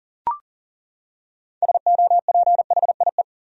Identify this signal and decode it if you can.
Morse code, heard as a keyed tone, about 40 wpm, 700 Hz. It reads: SOPHIE
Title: USA Common Names - 40wpm